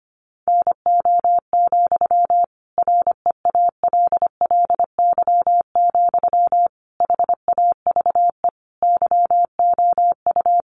Morse code, heard as a keyed tone, about 25 wpm, 700 Hz. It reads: NO, REALLY, HAVE YOU
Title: Sentences from Top 1000 Words 25wpm